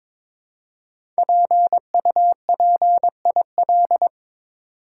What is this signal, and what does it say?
Morse code, heard as a keyed tone, about 23 wpm, 700 Hz. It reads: PUPIL
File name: 5 Letter Words - First 3 Letters Given - 22wpm